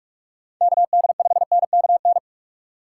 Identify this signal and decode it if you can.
Morse code, heard as a keyed tone, about 41 wpm, 700 Hz. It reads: KD5NKN
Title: USA Call Signs - 2x Character Spacing - 45wpm